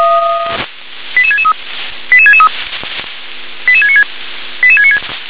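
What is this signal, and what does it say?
Five-tone selective call (ZVEI), heard as quick runs of five tones, tones of about 70 ms. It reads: 88663 88663 88696
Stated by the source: Received on a software-defined radio (SDR); ZVEI five-tone selective call 88717.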